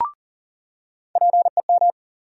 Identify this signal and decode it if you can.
Morse code, heard as a keyed tone, about 40 wpm, 700 Hz. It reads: PEM
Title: USA Call Sign Postfixes - 3 Letters 40wpm